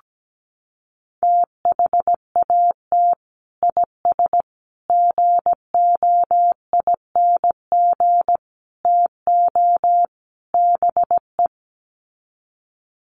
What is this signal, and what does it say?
Morse code, heard as a keyed tone, about 17 wpm, 700 Hz. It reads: THAT IS GOING TO BE
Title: Sets of 5 Words Encore2 17wpm